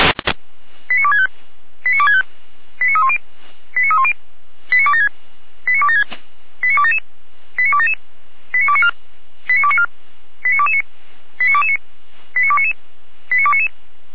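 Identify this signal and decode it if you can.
Five-tone selective call (ZVEI), heard as quick runs of five tones, tones of about 70 ms. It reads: 89265 89265 89310 89310 89276 89276 89277 89277 89294 89294 89229 89229 89299 89299
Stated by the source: Received on SDR, ZVEI five-tone call 89337, recorded by feuerwehr-orlamuende (fire brigade Orlamünde).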